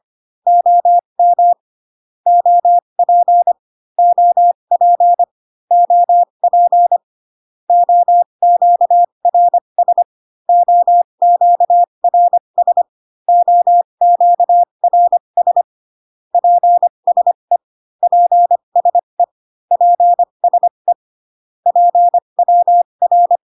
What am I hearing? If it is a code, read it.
Morse code, heard as a keyed tone, about 25 wpm, spaced as for 22 wpm, 700 Hz. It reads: OM OP OP OP OQRS OQRS OQRS PSE PSE PSE PWR
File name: CW common abbreviations alphabetical repeated 3 times 25 18wpm